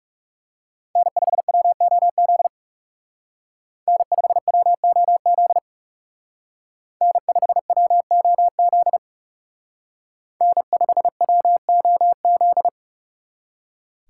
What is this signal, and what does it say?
Morse code, heard as a keyed tone, about 36 wpm, 700 Hz. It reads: N5WOZ N5WOZ N5WOZ N5WOZ
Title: Speed Racing - USA 1x3 Call Signs 30wpm